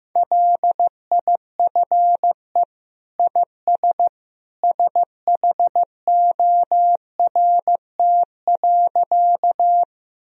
Morse code, heard as a keyed tone, about 15 wpm, 700 Hz. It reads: LIFE IS SHORT.
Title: Sentences from Top 1000 Words - Review Part 2 - 15wpm